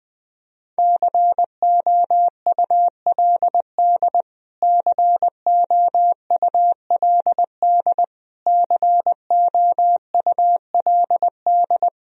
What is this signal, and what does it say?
Morse code, heard as a keyed tone, about 20 wpm, 700 Hz. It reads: COULD COULD COULD
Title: Top 100 Words Repeated 20wpm